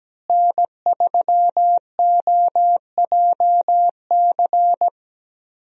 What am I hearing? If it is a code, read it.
Morse code, heard as a keyed tone, about 17 wpm, 700 Hz. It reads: N3OJC
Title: USA Call Signs - 5 Characters - Repeated 2x - 17wpm